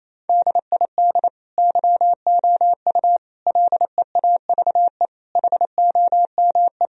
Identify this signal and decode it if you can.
Morse code, heard as a keyed tone, about 28 wpm, 700 Hz. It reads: DID YOU LEAVE HOME
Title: Sentences from Top 1000 Words - Review Part 4 - 28wpm